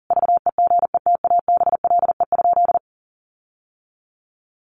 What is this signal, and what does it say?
Morse code, heard as a keyed tone, about 40 wpm, 700 Hz. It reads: VEGETABLE?
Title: Sentences from Top 1300 Words 40wpm